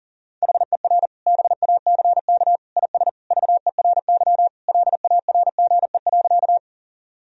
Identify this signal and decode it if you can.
Morse code, heard as a keyed tone, about 40 wpm, 700 Hz. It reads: HER BACK IS VERY LARGE.
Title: Sentences from Top 1000 Words - Review Part 1 - 40wpm